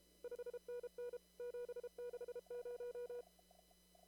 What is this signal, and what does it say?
Morse code, heard as a keyed tone, about 28 wpm, 490 Hz. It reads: 5NN 760
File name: CW / kt5j wpx cw-op-k5pi